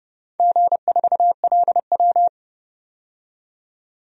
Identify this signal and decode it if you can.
Morse code, heard as a keyed tone, about 30 wpm, 700 Hz. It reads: G4LW
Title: DXCC-100 Call Signs - 4 Characters - 30wpm